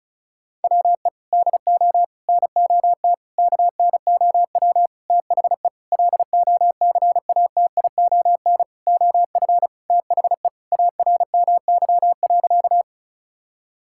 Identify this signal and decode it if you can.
Morse code, heard as a keyed tone, about 36 wpm, 700 Hz. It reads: WE DO NOT KNOW THE LOCATION OF THE ARMY.
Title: Sentences from Top 1000 Words - Review Part 6 - 35wpm